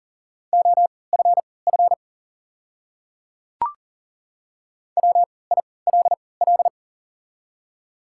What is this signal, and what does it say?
Morse code, heard as a keyed tone, about 38 wpm, spaced as for 22 wpm, 700 Hz. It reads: OFF WILL
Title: 3x Character Spacing - Top 300 Words - 40wpm